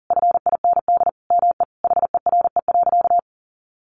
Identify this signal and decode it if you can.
Morse code, heard as a keyed tone, about 40 wpm, 700 Hz. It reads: FIND ME HERE.
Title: Sentences from Top 1200 Words 40wpm